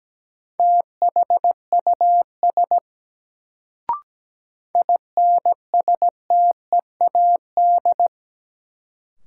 Morse code, heard as a keyed tone, about 17 wpm, 700 Hz. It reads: THUS INSTEAD
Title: Sentences from Top 800 Words - New Word Drill - 17wpm